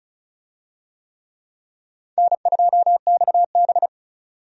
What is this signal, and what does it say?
Morse code, heard as a keyed tone, about 36 wpm, 700 Hz. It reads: N2XB
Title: USA Call Signs - 4 Characters - 35wpm